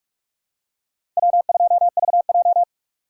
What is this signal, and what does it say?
Morse code, heard as a keyed tone, about 42 wpm, 700 Hz. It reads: W2VJ
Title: USA Call Signs - 4 Characters - 4x Character Spacing - 45wpm